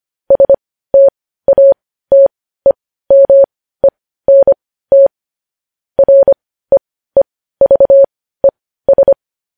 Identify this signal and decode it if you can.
Morse code, heard as a keyed tone, about 25 wpm, spaced as for 15 wpm, 550 Hz. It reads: STATEMENT REEVES
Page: news-headlines-in-morse-code-at-15-wpm